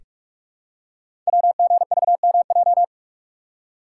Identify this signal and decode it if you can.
Morse code, heard as a keyed tone, about 44 wpm, 700 Hz. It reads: WGVMJ
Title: Code Groups - 5 Letters - 45wpm